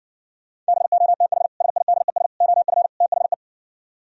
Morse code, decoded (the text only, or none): BOTH SIDES OF THE